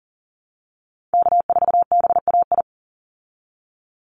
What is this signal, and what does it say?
Morse code, heard as a keyed tone, about 40 wpm, 700 Hz. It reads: K4BAI
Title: Call Signs Repeated 3x - 40wpm